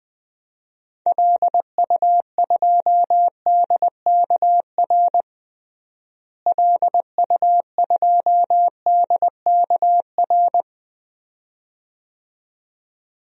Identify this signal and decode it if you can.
Morse code, heard as a keyed tone, about 20 wpm, 700 Hz. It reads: LU2DKR LU2DKR
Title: DXCC-100 Call Signs - 6 Characters - Repeated 2x - 20wpm